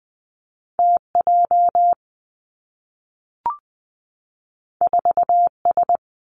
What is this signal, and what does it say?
Morse code, heard as a keyed tone, about 20 wpm, 700 Hz. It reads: T J 4 S
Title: Two Challenging Letter-Number Combinations 20wpm